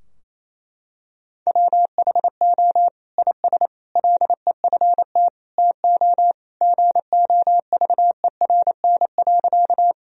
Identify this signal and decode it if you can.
Morse code, heard as a keyed tone, about 28 wpm, 690 Hz. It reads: WHO IS LEFT TO GOVERN.
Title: Sentences from Top 1000 Words - Review Part 2 - 28wpm